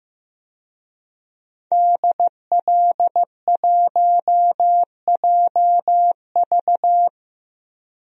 Morse code, heard as a keyed tone, about 15 wpm, 700 Hz. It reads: DL1JV
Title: DXCC-100 Call Signs - 5 Characters - 15wpm